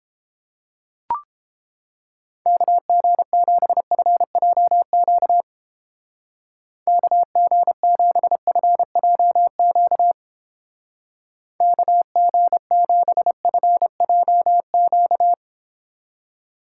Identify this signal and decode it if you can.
Morse code, heard as a keyed tone, about 31 wpm, 700 Hz. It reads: KG7FJQ KG7FJQ KG7FJQ